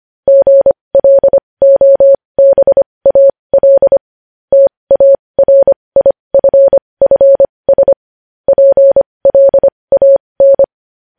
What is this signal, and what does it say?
Morse code, heard as a keyed tone, about 25 wpm, spaced as for 20 wpm, 550 Hz. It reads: GLOBAL TARIFFS PLAN
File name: podcast / News headlines-1x-25x20